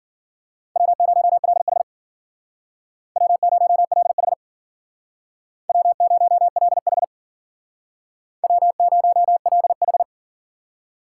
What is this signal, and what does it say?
Morse code, heard as a keyed tone, about 51 wpm, 700 Hz. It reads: W0LH W0LH W0LH W0LH